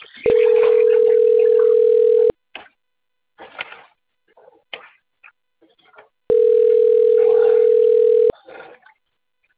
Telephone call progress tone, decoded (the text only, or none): ringback tone